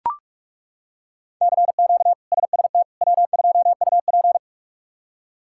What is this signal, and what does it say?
Morse code, heard as a keyed tone, about 42 wpm, 700 Hz. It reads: CQ SST W2UP
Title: K1USN SST QSO Practice - 45wpm